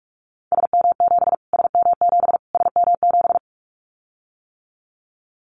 Morse code, heard as a keyed tone, about 44 wpm, 700 Hz. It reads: SM7 SM7 SM7
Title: Callsign Prefixes for Top 25 DXCC - Repeated 3x - 45wpm